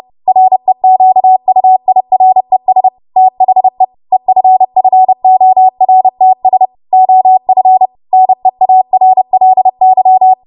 Morse code, heard as a keyed tone, about 30 wpm, 750 Hz. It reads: REQUIRES THE EFFORTS OF NEARLY
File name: Morse / 30WPM